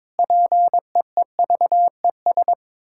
Morse code, heard as a keyed tone, about 22 wpm, 700 Hz. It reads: PEEVES
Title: Pangram Challenge v2 - 22wpm